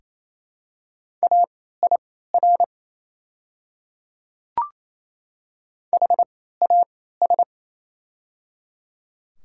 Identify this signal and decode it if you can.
Morse code, heard as a keyed tone, about 28 wpm, spaced as for 16 wpm, 700 Hz. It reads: AIR HAS
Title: 3x Character Spacing - Top 300 Words - 28wpm